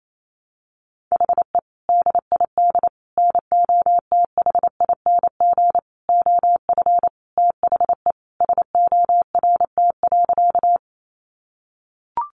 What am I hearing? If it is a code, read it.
Morse code, heard as a keyed tone, about 28 wpm, 700 Hz. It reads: HE DID NOTHING OF THE SORT.
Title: Sentences from Top 1300 Words 28wpm